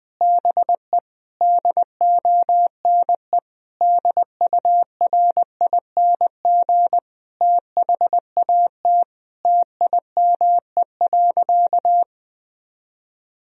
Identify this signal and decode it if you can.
Morse code, heard as a keyed tone, about 21 wpm, 700 Hz. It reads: BE DONE DURING THAT TIME.
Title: Sentences from Top 1000 Words - Review Part 2 - 20wpm